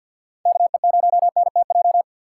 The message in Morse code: KE0NTJ

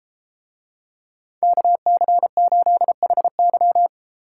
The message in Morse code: KC8HY

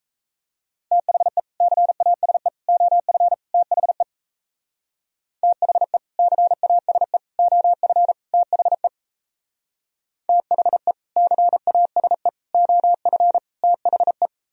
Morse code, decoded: THE CASE OF THE THE CASE OF THE THE CASE OF THE

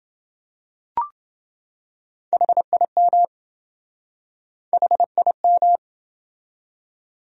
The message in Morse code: HIM HIM

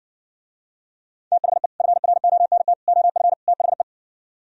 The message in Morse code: THE FRONT OF THE

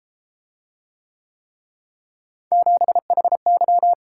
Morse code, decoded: 7HY

